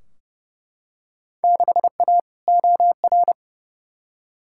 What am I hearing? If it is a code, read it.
Morse code, heard as a keyed tone, about 30 wpm, 700 Hz. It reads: 6A OR
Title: Field Day Exchanges - 30wpm